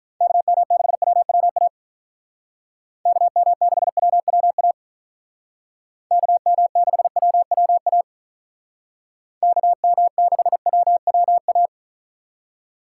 Morse code, heard as a keyed tone, about 43 wpm, 700 Hz. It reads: KM6WWA KM6WWA KM6WWA KM6WWA
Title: Speed Racing - USA 2x3 Call Signs 35wpm